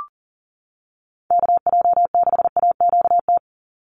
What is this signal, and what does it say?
Morse code, heard as a keyed tone, about 40 wpm, 700 Hz. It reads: KJ6AQT
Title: USA Call Signs 40wpm